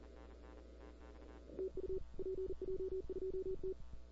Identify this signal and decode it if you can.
Morse code, heard as a keyed tone, about 40 wpm, 380 Hz. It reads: TU PJ2T
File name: CW / 2005-k5tr arrl cw dx -01:00